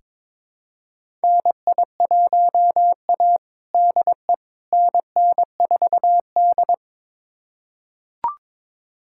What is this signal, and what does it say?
Morse code, heard as a keyed tone, about 22 wpm, 700 Hz. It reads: NI1A DE NN4D
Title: USA Callsign DE Callsign - 4 Characters - 22wpm